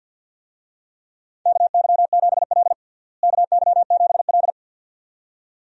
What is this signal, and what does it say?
Morse code, heard as a keyed tone, about 49 wpm, 700 Hz. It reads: KY7L KY7L
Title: USA Call Signs - 4 Characters - Repeated 2x - 50wpm